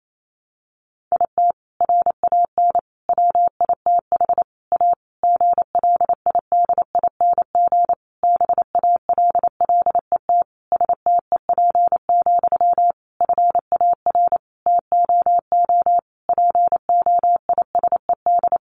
Morse code, heard as a keyed tone, about 28 wpm, 700 Hz. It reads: IT RAN WITH A GLIDING BALLET STEP, FAR TOO POISED
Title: A Sound Of Thunder 28wpm